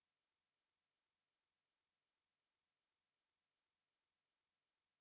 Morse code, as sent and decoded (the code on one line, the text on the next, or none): none